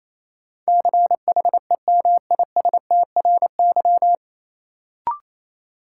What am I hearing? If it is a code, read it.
Morse code, heard as a keyed tone, about 28 wpm, 700 Hz. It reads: CHEMISTRY
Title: Sentences from Top 1000 Words - Review Part 6 - New Word Drill - 28wpm